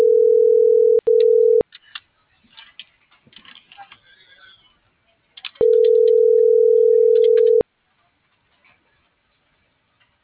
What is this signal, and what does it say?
Ringback tone (North America), 440 + 480 Hz.